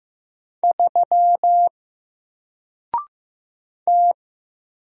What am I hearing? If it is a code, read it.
Morse code, heard as a keyed tone, about 15 wpm, 700 Hz. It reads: 3 T
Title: Single Letter-Number 15wpm